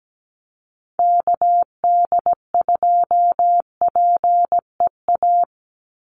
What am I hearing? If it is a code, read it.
Morse code, heard as a keyed tone, about 17 wpm, 700 Hz. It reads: KD2PEA
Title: USA Call Signs 17wpm